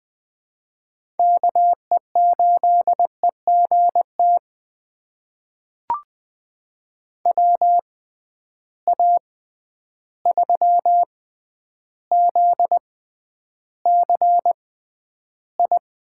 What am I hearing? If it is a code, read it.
Morse code, heard as a keyed tone, about 20 wpm, spaced as for 10 wpm, 700 Hz. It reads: KE8EGT WA3ZCI
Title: USA Call Signs - 6 Characters - 6x Character Spacing - 20wpm